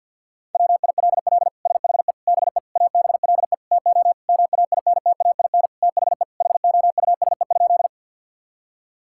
Morse code, heard as a keyed tone, about 50 wpm, 700 Hz. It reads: WILL SHE BE ABLE TO MAINTAIN THE HOUSE?
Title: Sentences from Top 1000 Words - Review Part 5 - 50wpm